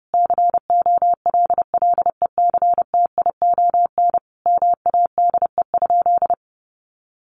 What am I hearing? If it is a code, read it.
Morse code, heard as a keyed tone, about 30 wpm, 700 Hz. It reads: COLLECTION MADE?
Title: Sentences from Top 1200 Words 30wpm